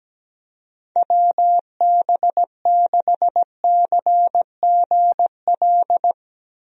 Morse code, heard as a keyed tone, about 17 wpm, 700 Hz. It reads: WB6CGL
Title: USA Call Signs - 6 Characters - Repeated 2x - 17wpm